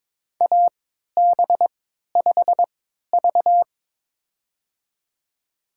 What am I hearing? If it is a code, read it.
Morse code, heard as a keyed tone, about 22 wpm, spaced as for 13 wpm, 700 Hz. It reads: AB5V
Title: USA Call Signs - 4 Characters - 3x Character Spacing - 22wpm